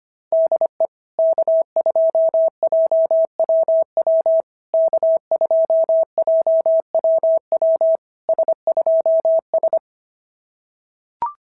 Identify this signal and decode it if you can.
Morse code, heard as a keyed tone, about 24 wpm, 650 Hz. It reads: DE K2JWW K2JWW S2S
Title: SOTA QSO Practice 25wpm